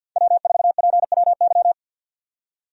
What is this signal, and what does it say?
Morse code, heard as a keyed tone, about 51 wpm, 700 Hz. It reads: W4PWY
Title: USA Call Signs - 5 Characters - 50wpm